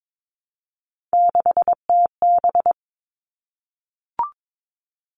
Morse code, heard as a keyed tone, about 22 wpm, 700 Hz. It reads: 6TB
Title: Three Challenging Letter-Number Combinations 22wpm